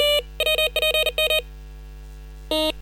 Morse code, heard as a keyed tone, about 40 wpm, 2500 Hz. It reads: WPM
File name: software / Morserino-32 / 40-WPM-VHSC